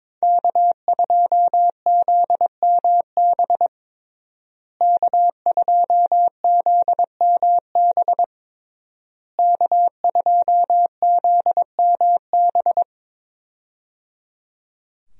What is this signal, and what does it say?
Morse code, heard as a keyed tone, about 22 wpm, 700 Hz. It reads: K2ZMB K2ZMB K2ZMB
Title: USA Call Signs - 5 Characters - Repeated 3x - 22wpm